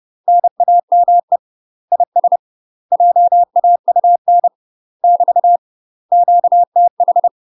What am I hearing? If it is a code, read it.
Morse code, heard as a keyed tone, about 30 wpm, 700 Hz. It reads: NAME IS JAUN <BT> QTH
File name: UK MASTER QSOs TRAINING / 30WPM / UK MASTER QSOs 0036